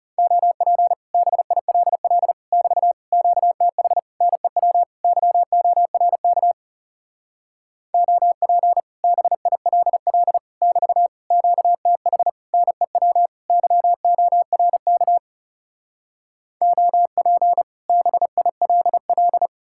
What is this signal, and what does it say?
Morse code, heard as a keyed tone, about 40 wpm, 700 Hz. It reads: OP BILL <BT> QTH NEW YORK OP BILL <BT> QTH NEW YORK OP BILL